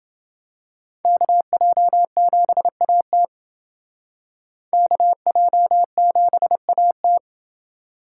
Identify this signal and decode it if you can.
Morse code, heard as a keyed tone, about 28 wpm, 700 Hz. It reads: KJ7AT KJ7AT